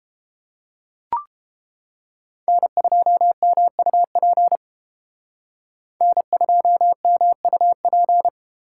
Morse code, heard as a keyed tone, about 31 wpm, 700 Hz. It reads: N2MUP N2MUP